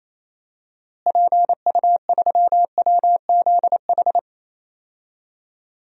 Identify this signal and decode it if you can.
Morse code, heard as a keyed tone, about 28 wpm, 700 Hz. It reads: PU3WZH
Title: DXCC-100 Call Signs - 6 Characters - Repeated 2x - 28wpm